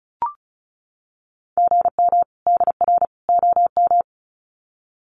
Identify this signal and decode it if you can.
Morse code, heard as a keyed tone, about 34 wpm, 700 Hz. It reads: GM DR OM
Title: Common QSO Phrases 35wpm